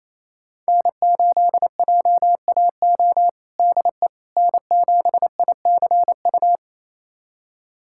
Morse code, heard as a keyed tone, about 28 wpm, 700 Hz. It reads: N8JAO DE N7ICU
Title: USA Callsign DE Callsign 28wpm